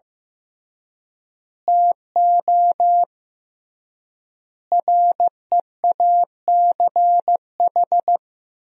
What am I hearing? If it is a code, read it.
Morse code, heard as a keyed tone, about 15 wpm, 700 Hz. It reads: TO REACH
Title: 3x Word-Spacing - Sentences from Top 400 Words - 15wpm